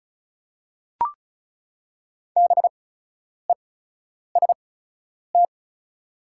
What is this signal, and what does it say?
Morse code, heard as a keyed tone, about 36 wpm, spaced as for 10 wpm, 700 Hz. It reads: BEST
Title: 8x Character Spacing - Top 500 Words - 35wpm